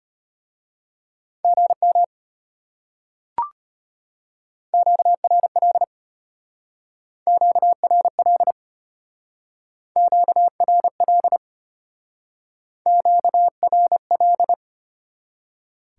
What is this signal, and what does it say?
Morse code, heard as a keyed tone, about 34 wpm, 700 Hz. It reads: GM QRL QRL QRL QRL